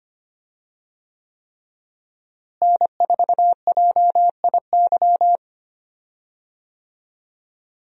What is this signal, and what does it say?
Morse code, heard as a keyed tone, about 25 wpm, 700 Hz. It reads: N4JIY
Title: USA Call Signs - 5 Characters - Repeated 3x - 25wpm